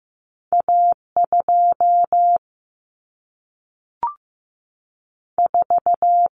Morse code, heard as a keyed tone, about 15 wpm, 700 Hz. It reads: A2 4
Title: Common Cut Numbers - 2-Digit 15wpm